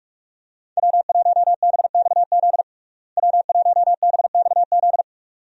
Morse code, heard as a keyed tone, about 43 wpm, 700 Hz. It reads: W1BXZ W1BXZ
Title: USA Call Signs - 5 Characters - Repeated 2x - 45wpm